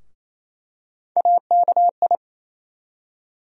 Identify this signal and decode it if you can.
Morse code, heard as a keyed tone, about 28 wpm, 700 Hz. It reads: AKI
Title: USA Call Sign Postfixes - 3 Letters 28wpm